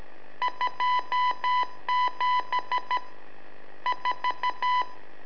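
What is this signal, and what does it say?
Morse code, heard as a keyed tone, about 13 wpm, 1000 Hz. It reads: 27 4
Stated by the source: Telemetry in Morse from LU7AA and LU4EZ, AMSAT and Turma Mar del Plata, sent from a balloon.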